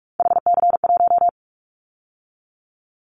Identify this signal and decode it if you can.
Morse code, heard as a keyed tone, about 46 wpm, 700 Hz. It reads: HC1
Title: Callsign Prefixes for Top 100 DXCC - 45wpm